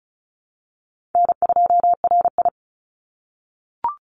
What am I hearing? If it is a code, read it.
Morse code, heard as a keyed tone, about 36 wpm, 700 Hz. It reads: N2RI
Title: Call Signs - 35wpm